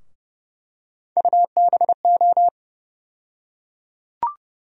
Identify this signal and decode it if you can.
Morse code, heard as a keyed tone, about 32 wpm, 700 Hz. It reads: UBO
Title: USA Call Sign Postfixes - 3 Letters 30wpm